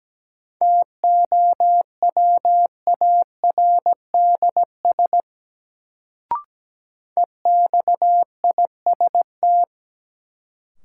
Morse code, heard as a keyed tone, about 17 wpm, 700 Hz. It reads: TOWARDS EXIST
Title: Sentences from Top 1100 Words - New Word Drill Alt - 17wpm